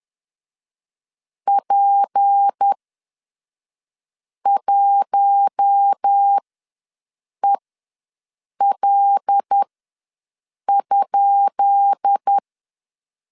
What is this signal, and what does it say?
Morse code, heard as a keyed tone, about 10 wpm, spaced as for 5 wpm, 800 Hz. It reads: P1EL?
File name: LEZIONI / 10 SIGNS / 10B INT ws 10wpm 4134